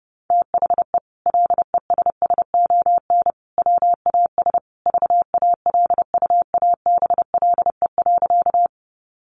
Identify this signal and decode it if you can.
Morse code, heard as a keyed tone, about 30 wpm, 700 Hz. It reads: THE LESSON WAS VALUABLE.
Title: Sentences from Top 1200 Words 30wpm